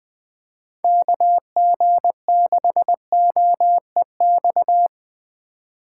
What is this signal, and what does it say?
Morse code, heard as a keyed tone, about 20 wpm, 700 Hz. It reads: KG6OEX
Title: Speed Racing - USA 2x3 Call Signs 17wpm